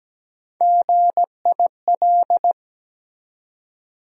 Morse code, heard as a keyed tone, about 17 wpm, 700 Hz. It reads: GIL
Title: USA Call Sign Postfixes - 3 Letters 17wpm